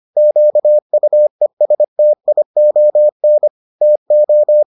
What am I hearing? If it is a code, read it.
Morse code, heard as a keyed tone, about 25 wpm, 600 Hz. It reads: QUESTION TO